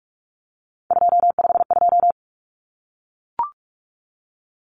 Morse code, heard as a keyed tone, about 44 wpm, 700 Hz. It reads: 252
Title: Common Cut Numbers - 3-Digit 45wpm